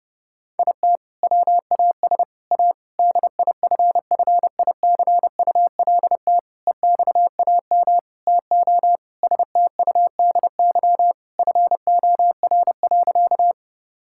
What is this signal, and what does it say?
Morse code, heard as a keyed tone, about 30 wpm, 700 Hz. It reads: IT WAS A DIFFICULT EXAM TO STUDY FOR.
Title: Sentences from Top 1000 Words - Review Part 5 - 30wpm